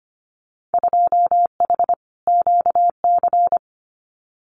Morse code, heard as a keyed tone, about 25 wpm, 700 Hz. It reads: 2H QC
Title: Winter Field Day Exchanges Encore 25wpm